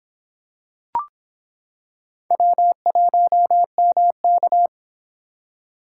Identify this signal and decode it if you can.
Morse code, heard as a keyed tone, about 26 wpm, 700 Hz. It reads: W1MK